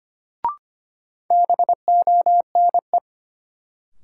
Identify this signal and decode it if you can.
Morse code, heard as a keyed tone, about 25 wpm, 700 Hz. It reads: BONE
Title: Sentences from Top 1600 Words - New Word Drill - 25wpm